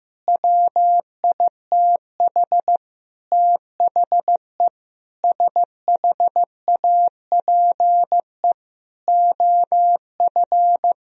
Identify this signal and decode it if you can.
Morse code, heard as a keyed tone, about 15 wpm, 700 Hz. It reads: WITH THE SHAPE OF
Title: Sentences from Top 1000 Words - Review Part 3 - 15wpm